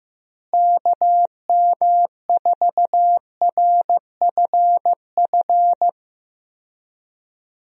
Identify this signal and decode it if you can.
Morse code, heard as a keyed tone, about 16 wpm, 700 Hz. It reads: KM4RFF
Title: USA Call Signs - 6 Characters - Repeated 3x - 15wpm